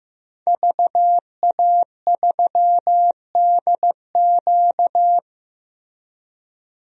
Morse code, heard as a keyed tone, about 15 wpm, 700 Hz. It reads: VA3DQ
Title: DXCC-100 Call Signs - 5 Characters - Repeated 3x - 15wpm